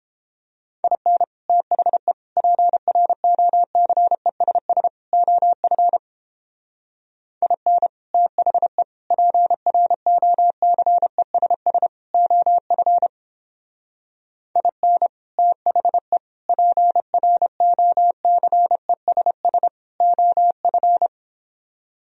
Morse code, heard as a keyed tone, about 29 wpm, 700 Hz. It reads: IN THE PROCESS OF IN THE PROCESS OF IN THE PROCESS OF